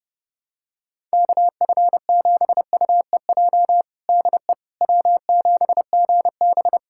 Morse code, heard as a keyed tone, about 30 wpm, 700 Hz. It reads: KF7UEJ DE W7GB
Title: Practice QSO - Exchange 4 - Ending - 30wpm